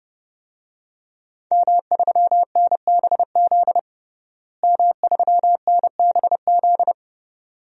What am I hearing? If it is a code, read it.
Morse code, heard as a keyed tone, about 31 wpm, 700 Hz. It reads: M3NBZ M3NBZ
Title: DXCC-100 Call Signs - 5 Characters - Repeated 2x - 30wpm